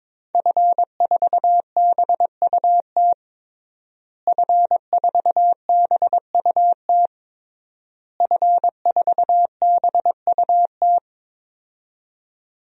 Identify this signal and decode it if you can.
Morse code, heard as a keyed tone, about 22 wpm, 700 Hz. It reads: F4BUT F4BUT F4BUT
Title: DXCC-100 Call Signs - 5 Characters - Repeated 3x - 22wpm